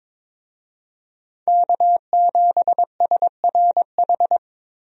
Morse code, heard as a keyed tone, about 22 wpm, 700 Hz. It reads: K7SRH
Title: USA Call Signs - 5 Characters - Repeated 3x - 22wpm